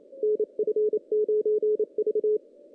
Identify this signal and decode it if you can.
Morse code, heard as a keyed tone, about 28 wpm, 430 Hz. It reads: NF9V
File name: 2016 JAN CW NAQP / 2016 Jan naqp k5ot - 0-5